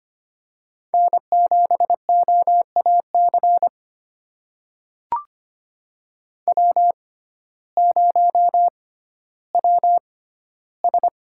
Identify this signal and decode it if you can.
Morse code, heard as a keyed tone, about 25 wpm, spaced as for 14 wpm, 700 Hz. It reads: N7OAC W0WS
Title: USA Call Signs - 6x Character Spacing - 25wpm